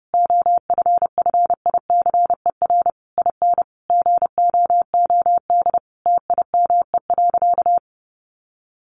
Morse code, heard as a keyed tone, about 30 wpm, 700 Hz. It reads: OFFICER IN GOOD TIME.
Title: Sentences from Top 1200 Words 30wpm